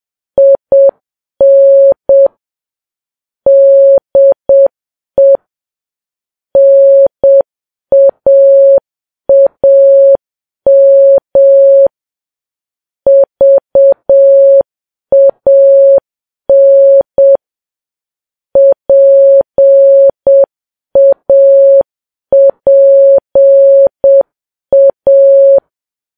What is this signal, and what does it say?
Morse code, heard as a keyed tone, about 7 wpm, 550 Hz. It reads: IN DE NAAM VAN PAPA
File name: Files Web tls / Morse on9fv / morse7kps